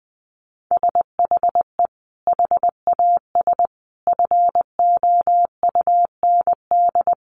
Morse code, heard as a keyed tone, about 20 wpm, 700 Hz. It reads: SHE HAS FOUND
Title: Sentences from Top 1500 Words 20wpm